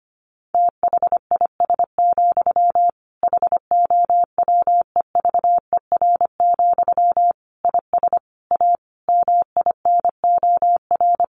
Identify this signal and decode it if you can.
Morse code, heard as a keyed tone, about 25 wpm, 700 Hz. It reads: THIS, HOWEVER, IS A MINOR
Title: Sentences from Top 1500 Words 25wpm